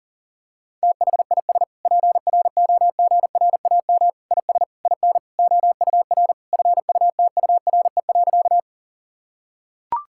Morse code, heard as a keyed tone, about 40 wpm, 700 Hz. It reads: THIS PROGRAM IS IN OUR FUTURE.
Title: Sentences from Top 1000 Words - Review Part 5 - 40wpm